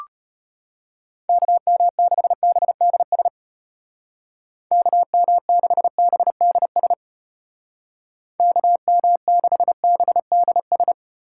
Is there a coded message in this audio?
Yes: Morse code from 1.3 to 10.9 s, heard as a keyed tone, about 33 wpm, 700 Hz. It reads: KM6BDS KM6BDS KM6BDS